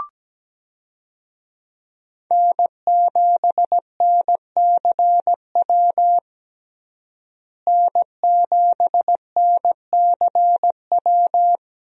Morse code, heard as a keyed tone, about 17 wpm, 700 Hz. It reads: N7NCW N7NCW
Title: USA Call Signs - 5 Characters - Repeated 2x - 17wpm